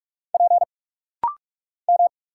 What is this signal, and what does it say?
Morse code, heard as a keyed tone, about 46 wpm, 700 Hz. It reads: P M
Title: Single Letters - Rapid-Fire 45wpm